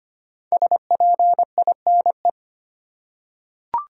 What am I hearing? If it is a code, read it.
Morse code, heard as a keyed tone, about 25 wpm, 700 Hz. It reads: SPINE
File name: 5 Letter Words - First 3 Letters Given - 25wpm